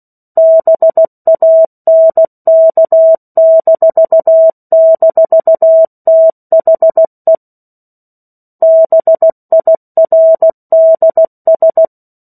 Morse code, heard as a keyed tone, about 16 wpm, 650 Hz. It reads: BANK--THE BIRDS
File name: 03-Chapter-03 16 WPM